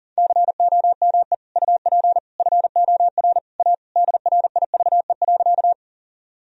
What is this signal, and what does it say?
Morse code, heard as a keyed tone, about 37 wpm, 700 Hz. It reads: COME UP FOR A DRIVE.
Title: Sentences from Top 1000 Words - Review Part 2 - 40wpm